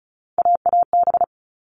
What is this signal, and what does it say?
Morse code, heard as a keyed tone, about 34 wpm, 700 Hz. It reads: AAB
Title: All Cut Numbers - 3-Digit 35wpm